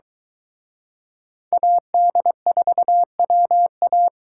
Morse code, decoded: AD4WA

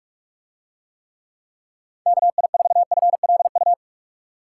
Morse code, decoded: KI4FLU

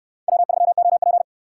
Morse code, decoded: W3OP

